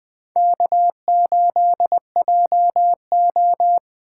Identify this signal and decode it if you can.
Morse code, heard as a keyed tone, about 20 wpm, 700 Hz. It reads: K8JO